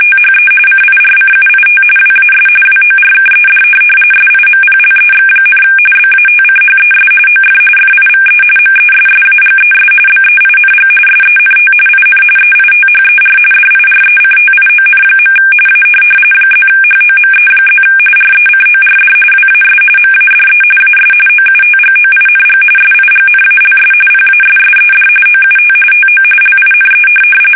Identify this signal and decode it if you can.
RTTY, about 45.45 baud. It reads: THE CIRCUIT WAS WORKED SIMPLEX, BATCHES OF FIVE MESSAGES BEING SENT IN EITHER DIRECTION. A +HOME+ RECORD WAS MADE AS OUTGOING SIGNALS PASSED OUT TO LINE